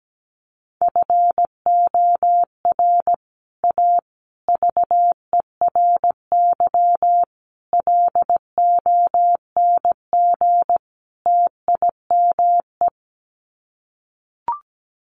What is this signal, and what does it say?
Morse code, heard as a keyed tone, about 17 wpm, 700 Hz. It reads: FOR A VERY LONG TIME
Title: Sets of 5 Words 17wpm